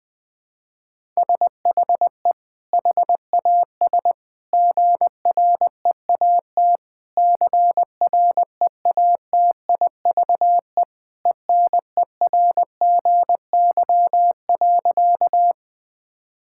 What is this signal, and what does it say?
Morse code, heard as a keyed tone, about 20 wpm, 700 Hz. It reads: SHE HAS GREAT CREATIVE ENERGY.
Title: Sentences from Top 1000 Words - Review Part 6 - 20wpm